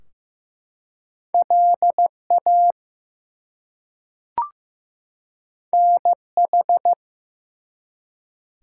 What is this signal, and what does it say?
Morse code, heard as a keyed tone, about 15 wpm, 700 Hz. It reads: LA NH
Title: US States and Canadian Provinces 15wpm